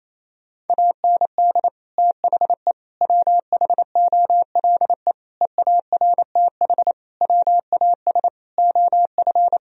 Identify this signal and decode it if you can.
Morse code, heard as a keyed tone, about 28 wpm, 700 Hz. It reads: AND THE WHOLE EARTH WAS OF
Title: Sentences from Top 1000 Words - Review Part 3 - 28wpm